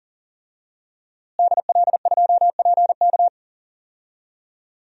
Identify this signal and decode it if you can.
Morse code, heard as a keyed tone, about 40 wpm, 700 Hz. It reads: DL2PK
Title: DXCC-100 Call Signs - 5 Characters - Repeated 3x - 40wpm